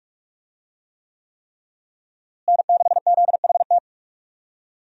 Morse code, heard as a keyed tone, about 46 wpm, 700 Hz. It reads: N6ZHT
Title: USA Call Signs - 5 Characters - Repeated 3x - 45wpm